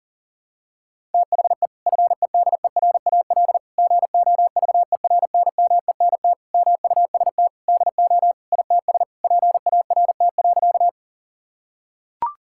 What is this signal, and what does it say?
Morse code, heard as a keyed tone, about 40 wpm, 700 Hz. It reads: THE FEDERAL GOVERNMENT MUST DO ITS PART.
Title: Sentences from Top 1000 Words - Review Part 5 - 40wpm